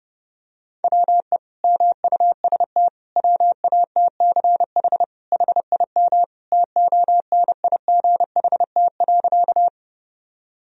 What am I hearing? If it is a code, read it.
Morse code, heard as a keyed tone, about 30 wpm, 700 Hz. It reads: WE MUST WATCH HIM TONIGHT.
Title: Sentences from Top 1000 Words - Review Part 3 - 30wpm